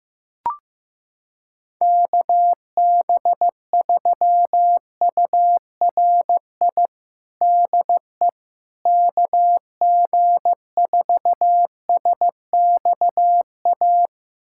Morse code, heard as a keyed tone, about 15 wpm, 700 Hz. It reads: KB3URI DE KG4SXA
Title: USA Callsign DE Callsign 15wpm